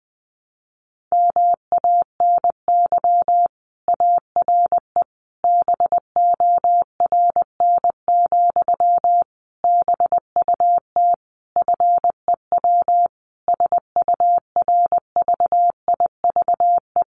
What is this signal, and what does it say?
Morse code, heard as a keyed tone, about 20 wpm, 700 Hz. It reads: MANY ARE BORN, BUT FEW SURVIVE
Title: Sentences from Top 1300 Words 20wpm